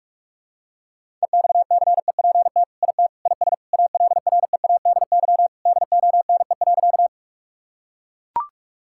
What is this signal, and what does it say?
Morse code, heard as a keyed tone, about 42 wpm, 700 Hz. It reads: EXCEPT IT IS ALREADY DONE.
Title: Sentences from Top 1600 Words 45wpm